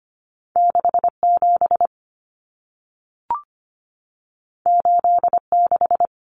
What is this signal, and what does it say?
Morse code, heard as a keyed tone, about 25 wpm, 700 Hz. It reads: 67 86
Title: Numbers - 2-Digit 25wpm